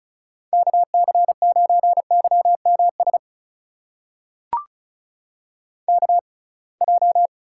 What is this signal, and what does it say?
Morse code, heard as a keyed tone, about 36 wpm, spaced as for 27 wpm, 700 Hz. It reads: KC9YMS KJ
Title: USA Call Signs - 6 Characters - 6x Character Spacing - 35wpm